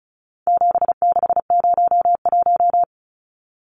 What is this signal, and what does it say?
Morse code, heard as a keyed tone, about 34 wpm, 700 Hz. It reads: 7601
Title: Numbers - 4-Digit 35wpm